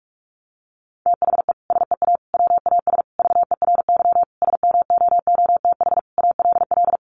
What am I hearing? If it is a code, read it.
Morse code, heard as a keyed tone, about 44 wpm, 700 Hz. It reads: THE SEA WAS VERY SMOOTH ALL
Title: Sentences from Top 1400 Words 45wpm